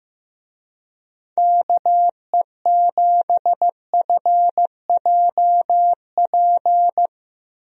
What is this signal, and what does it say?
Morse code, heard as a keyed tone, about 15 wpm, 700 Hz. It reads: KE7FJP
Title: USA Call Signs - 6 Characters - Repeated 3x - 15wpm